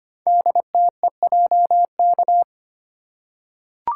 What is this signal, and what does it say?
Morse code, heard as a keyed tone, about 25 wpm, 700 Hz. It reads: DTEJK
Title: Code Groups - 5 Letters - 25wpm